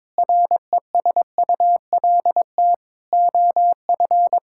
Morse code, heard as a keyed tone, about 22 wpm, 700 Hz. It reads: RESULT OF
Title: Sentences from Top 1000 Words - Review Part 5 - 22wpm